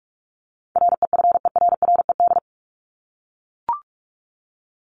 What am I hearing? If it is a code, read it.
Morse code, heard as a keyed tone, about 44 wpm, 700 Hz. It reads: REFERRED
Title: ED Suffix - Top 100 Words with Base Words 45wpm